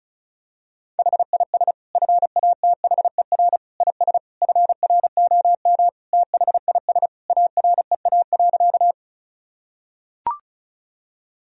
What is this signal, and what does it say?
Morse code, heard as a keyed tone, about 36 wpm, 700 Hz. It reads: HIS FATHER IS FROM THIS AREA.
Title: Sentences from Top 1000 Words - Review Part 2 - 35wpm